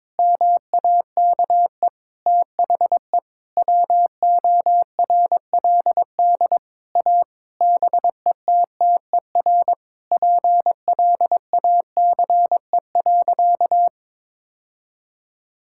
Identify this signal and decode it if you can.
Morse code, heard as a keyed tone, about 22 wpm, 700 Hz. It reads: MAKE THE WORLD A BETTER PLACE.
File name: Sentences with 7-Word Phrases - 22wpm